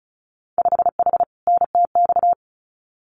Morse code, heard as a keyed tone, about 34 wpm, 700 Hz. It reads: 5H NTX
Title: Winter Field Day Exchanges Encore 35wpm